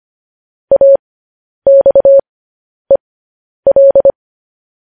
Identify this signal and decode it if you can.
Morse code, heard as a keyed tone, about 25 wpm, spaced as for 10 wpm, 550 Hz. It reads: AXEL
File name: podcast / News headlines-1x-25x10